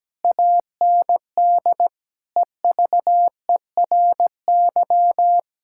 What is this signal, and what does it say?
Morse code, heard as a keyed tone, about 17 wpm, 700 Hz. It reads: AND EVERY
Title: Sentences from Top 1600 Words 17wpm